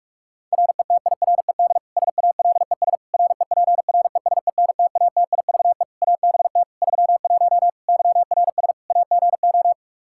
Morse code, heard as a keyed tone, about 46 wpm, 700 Hz. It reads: RETIRED SALES REPRESENTATIVE ABT 31 YRS AGO
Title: Practice QSO - Exchange 3 - More Info - 45wpm